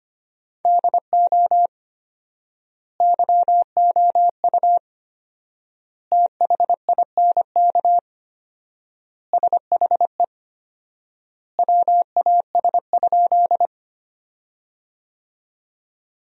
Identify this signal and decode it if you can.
Morse code, heard as a keyed tone, about 25 wpm, 700 Hz. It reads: DO YOU THINK SHE WAS?
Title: 4x Word-Spacing - Sentences from Top 200 Words - 25wpm